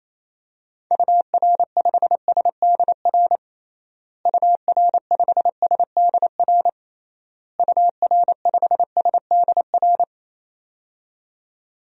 Morse code, heard as a keyed tone, about 28 wpm, 700 Hz. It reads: UR5SDR UR5SDR UR5SDR
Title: DXCC-100 Call Signs - 6 Characters - Repeated 3x - 28wpm